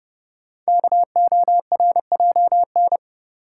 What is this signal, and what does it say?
Morse code, heard as a keyed tone, about 30 wpm, 700 Hz. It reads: KORJN
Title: Code Groups - 5 Letters - 30wpm